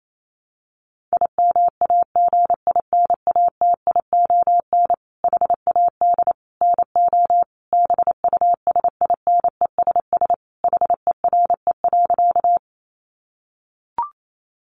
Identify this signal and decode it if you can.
Morse code, heard as a keyed tone, about 28 wpm, 700 Hz. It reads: IMAGINATION HAD NO BUSINESS HERE.
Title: Sentences from Top 1300 Words 28wpm